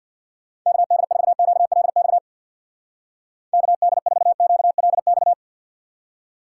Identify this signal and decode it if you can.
Morse code, heard as a keyed tone, about 62 wpm, 700 Hz. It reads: KD4QLX KD4QLX
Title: Speed Racing - USA 2x3 Call Signs 50wpm